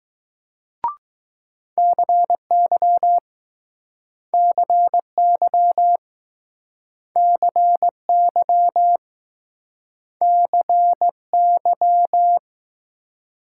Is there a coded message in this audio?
Yes: Morse code from 1.8 to 12.4 s, heard as a keyed tone, about 20 wpm, 700 Hz. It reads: CY CY CY CY